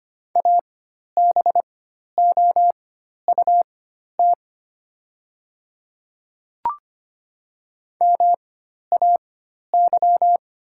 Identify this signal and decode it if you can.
Morse code, heard as a keyed tone, about 25 wpm, spaced as for 12 wpm, 700 Hz. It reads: ABOUT MAY
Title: 4x Character Spacing - Top 100 Words - 25wpm